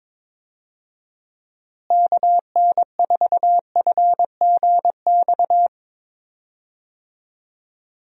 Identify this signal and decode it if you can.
Morse code, heard as a keyed tone, about 21 wpm, 700 Hz. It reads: KN4FGX
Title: USA Call Signs - 6 Characters - Repeated 2x - 22wpm